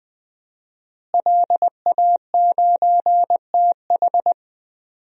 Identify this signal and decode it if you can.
Morse code, heard as a keyed tone, about 20 wpm, 700 Hz. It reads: LA9TH
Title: DXCC-100 Call Signs - 5 Characters - Repeated 3x - 20wpm